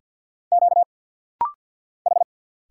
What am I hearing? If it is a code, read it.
Morse code, heard as a keyed tone, about 46 wpm, 700 Hz. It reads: Q H
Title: Single Letter-Number Alt - Rapid-Fire 50wpm